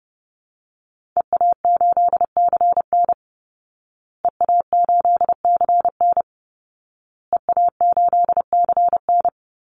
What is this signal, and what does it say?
Morse code, heard as a keyed tone, about 30 wpm, 700 Hz. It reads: EA8CN EA8CN EA8CN
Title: Call Signs Repeated 2x - 30wpm